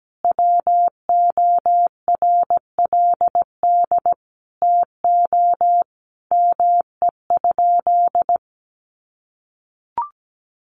Morse code, heard as a keyed tone, about 17 wpm, 700 Hz. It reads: WORLD TO ME?
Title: Sentences from Top 900 Words 17wpm